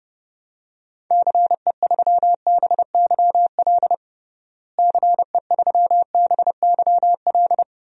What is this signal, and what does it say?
Morse code, heard as a keyed tone, about 32 wpm, 700 Hz. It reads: CE3BYL CE3BYL
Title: DXCC-100 Call Signs - 6 Characters - Repeated 3x - 30wpm